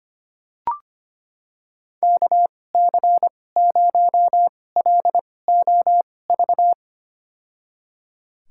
Morse code, heard as a keyed tone, about 25 wpm, spaced as for 18 wpm, 700 Hz. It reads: KC0LOV
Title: USA Call Signs - 6 Characters - 2x Character Spacing - 25wpm